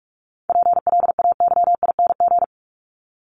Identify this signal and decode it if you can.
Morse code, heard as a keyed tone, about 44 wpm, 700 Hz. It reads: PLAYING
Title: ING Suffix - Top 100 Words with Base Words 45wpm